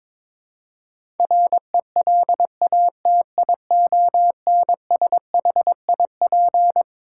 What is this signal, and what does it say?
Morse code, heard as a keyed tone, about 22 wpm, 700 Hz. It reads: RELATIONSHIP
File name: Twelve Letter Words - 22wpm